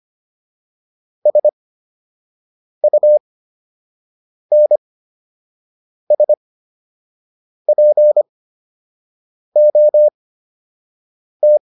Morse code, heard as a keyed tone, about 25 wpm, spaced as for 6 wpm, 600 Hz. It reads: SUNSPOT